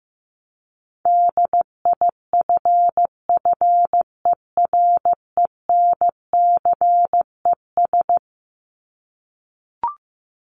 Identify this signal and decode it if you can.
Morse code, heard as a keyed tone, about 15 wpm, 700 Hz. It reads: DIFFERENCES
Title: ES:S Suffix - Top 100 Words with Base Words 15wpm